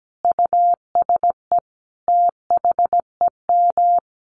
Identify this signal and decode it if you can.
Morse code, heard as a keyed tone, about 17 wpm, 700 Hz. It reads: USE THEM
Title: Sentences from Top 1300 Words 17wpm